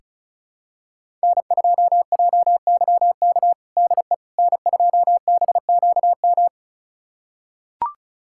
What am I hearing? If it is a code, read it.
Morse code, heard as a keyed tone, about 36 wpm, 700 Hz. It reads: N2JYK DE N2BQM
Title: USA Callsign DE Callsign - 5 Characters - 35wpm